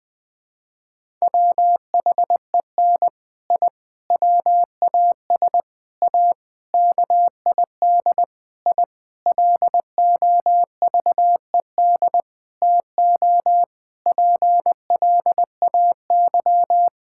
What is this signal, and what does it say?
Morse code, heard as a keyed tone, about 21 wpm, 700 Hz. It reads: WHEN I WAS A KID I LOVED TO PLAY
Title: Sentences with 5-Word Phrases - 20wpm